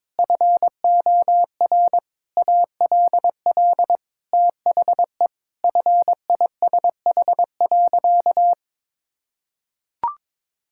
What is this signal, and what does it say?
Morse code, heard as a keyed tone, about 22 wpm, 700 Hz. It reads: FOR ALL THE FISH.
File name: practice / 22wpm / Sentences with 8-Word Phrases - 22wpm